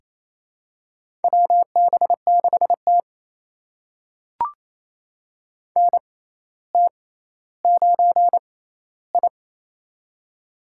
Morse code, heard as a keyed tone, about 28 wpm, spaced as for 14 wpm, 700 Hz. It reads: WB6T NT9I
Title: USA Call Signs - 6x Character Spacing - 28wpm